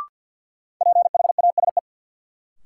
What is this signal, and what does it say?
Morse code, heard as a keyed tone, about 52 wpm, 700 Hz. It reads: PHASE
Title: Sentences from Top 1200 Words - New Word Drill - 50wpm